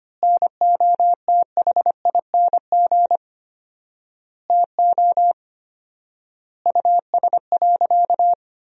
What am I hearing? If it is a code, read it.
Morse code, heard as a keyed tone, about 25 wpm, 700 Hz. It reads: NOTHING TO US.
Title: 4x Word-Spacing - Sentences from Top 500 Words - 25wpm